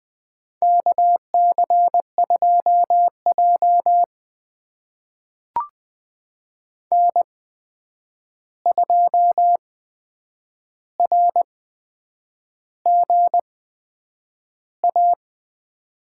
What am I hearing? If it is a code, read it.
Morse code, heard as a keyed tone, about 21 wpm, spaced as for 8 wpm, 700 Hz. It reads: KC2J N2RGA
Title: USA Call Signs - 8x Character Spacing - 20wpm